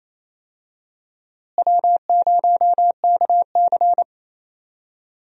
Morse code, heard as a keyed tone, about 28 wpm, 700 Hz. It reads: W0KC
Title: USA Call Signs - 4 Characters - Repeated 3x - 28wpm